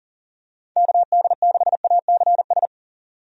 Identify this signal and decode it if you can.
Morse code, heard as a keyed tone, about 40 wpm, 700 Hz. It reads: KD6ACS
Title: USA Call Signs - 6 Characters - 40wpm